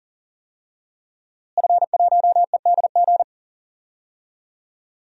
Morse code, heard as a keyed tone, about 40 wpm, 700 Hz. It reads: F1EDG
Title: DXCC-100 Call Signs - 5 Characters - Repeated 2x - 40wpm